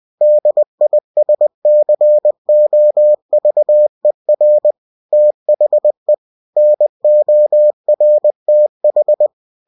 Morse code, heard as a keyed tone, about 20 wpm, 600 Hz. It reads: DISCOVER THE NORTH